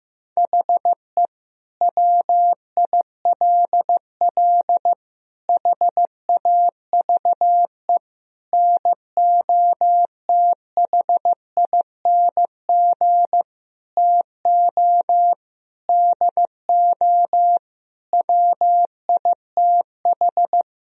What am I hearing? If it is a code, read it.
Morse code, heard as a keyed tone, about 15 wpm, 700 Hz. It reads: HE WILL HAVE NOTHING TO DO WITH
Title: Sentences from Top 1000 Words - Review Part 4 - 15wpm